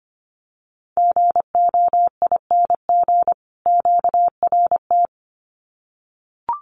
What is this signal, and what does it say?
Morse code, heard as a keyed tone, about 25 wpm, 700 Hz. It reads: GOING QRT
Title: Common QSO Phrases 25wpm